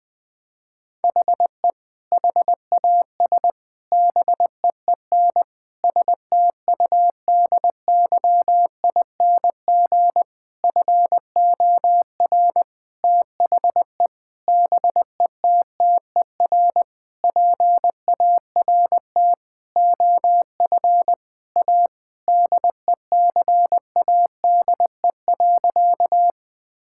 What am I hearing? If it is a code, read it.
Morse code, heard as a keyed tone, about 21 wpm, 700 Hz. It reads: HE HAS BEEN STUDYING FOR THE BETTER PART OF A DECADE.
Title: Sentences with 7-Word Phrases - 20wpm